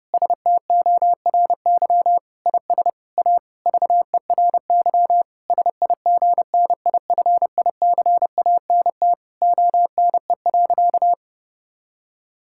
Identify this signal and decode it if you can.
Morse code, heard as a keyed tone, about 30 wpm, 700 Hz. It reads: STORY IS A VERY SIGNIFICANT ONE.
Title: Sentences from Top 1000 Words - Review Part 6 - 30wpm